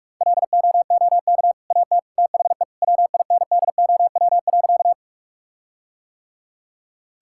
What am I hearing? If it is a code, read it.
Morse code, heard as a keyed tone, about 46 wpm, 700 Hz. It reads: LOOK AT THE WINDOW.
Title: Sentences from Top 1600 Words 45wpm